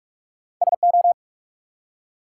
Morse code, heard as a keyed tone, about 44 wpm, 700 Hz. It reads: SO
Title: Two Letter Words - 45wpm